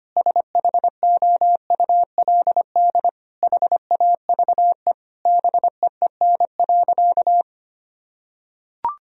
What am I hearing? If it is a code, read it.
Morse code, heard as a keyed tone, about 25 wpm, 700 Hz. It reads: SHOULD HAVE BEEN.
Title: Sentences from Top 1000 Words - Review Part 4 - 25wpm